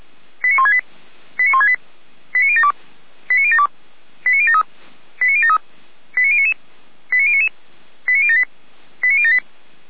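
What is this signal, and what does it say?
Five-tone selective call (ZVEI), heard as quick runs of five tones, tones of about 70 ms. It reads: 89168 89168 89072 89072 89073 89073 89099 89099 89078 89078
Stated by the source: ZVEI five-tone call 89150 received on an SDR.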